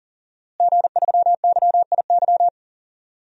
Morse code, heard as a keyed tone, about 38 wpm, 700 Hz. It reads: G3YIY
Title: DXCC-100 Call Signs - 5 Characters - Repeated 2x - 40wpm